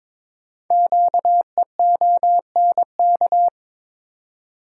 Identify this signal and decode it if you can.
Morse code, heard as a keyed tone, about 22 wpm, 700 Hz. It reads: QEONK
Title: Code Groups - 5 Letters - 22wpm